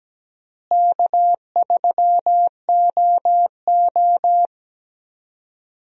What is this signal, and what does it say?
Morse code, heard as a keyed tone, about 17 wpm, 700 Hz. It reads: K3OO
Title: USA Call Signs - 4 Characters - Repeated 3x - 17wpm